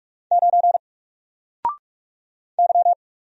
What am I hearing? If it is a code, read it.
Morse code, heard as a keyed tone, about 44 wpm, 700 Hz. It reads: 9 Y
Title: Single Letter-Number Alt 45wpm